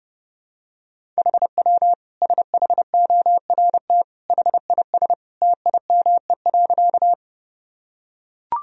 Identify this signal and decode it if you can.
Morse code, heard as a keyed tone, about 32 wpm, 700 Hz. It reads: HW SHORT HIS TIME.
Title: Sentences from Top 400 Words and Abbreviations 30wpm